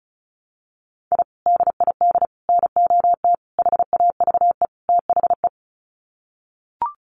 Morse code, heard as a keyed tone, about 34 wpm, 700 Hz. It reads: I DID NOT HAVE THE
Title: Sets of 5 Words Encore 35wpm